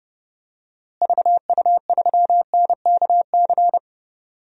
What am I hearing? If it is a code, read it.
Morse code, heard as a keyed tone, about 32 wpm, 700 Hz. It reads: VU3NKC
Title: DXCC-100 Call Signs - 6 Characters - Repeated 3x - 30wpm